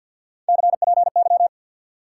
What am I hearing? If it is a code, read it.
Morse code, heard as a keyed tone, about 51 wpm, 700 Hz. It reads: CPY
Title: Top 180 QSO Elements 50wpm